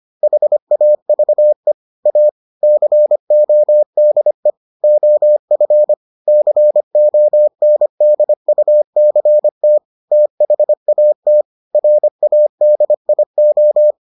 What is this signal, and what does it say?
Morse code, heard as a keyed tone, about 25 wpm, 600 Hz. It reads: HAVE A CODE OF CONDUCT THAT RADIO